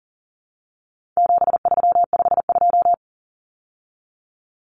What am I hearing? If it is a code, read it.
Morse code, heard as a keyed tone, about 40 wpm, 700 Hz. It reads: 7352
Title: Numbers - 4-Digit 40wpm